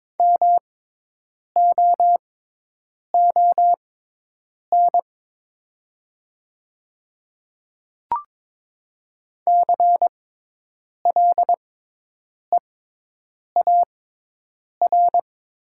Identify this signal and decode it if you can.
Morse code, heard as a keyed tone, about 22 wpm, spaced as for 8 wpm, 700 Hz. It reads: MOON CLEAR
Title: 6x Character Spacing - Top 500 Words - 22wpm